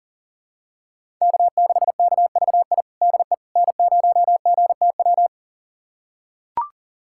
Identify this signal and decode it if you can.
Morse code, heard as a keyed tone, about 40 wpm, 700 Hz. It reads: K6KVI DE N0GTW
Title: USA Callsign DE Callsign - 5 Characters - 40wpm